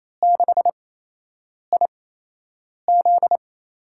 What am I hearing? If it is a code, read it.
Morse code, heard as a keyed tone, about 28 wpm, spaced as for 8 wpm, 700 Hz. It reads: 6IZ
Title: USA Call Signs - 4 Characters - 8x Character Spacing - 28wpm